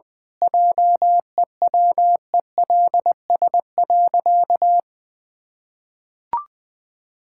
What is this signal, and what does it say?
Morse code, heard as a keyed tone, about 20 wpm, 700 Hz. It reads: JEWELS.
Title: Pangram Challenge v2 - 20wpm